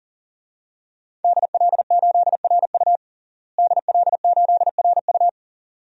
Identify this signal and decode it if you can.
Morse code, heard as a keyed tone, about 37 wpm, 700 Hz. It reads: DL8RU DL8RU
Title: DXCC-100 Call Signs - 5 Characters - Repeated 2x - 40wpm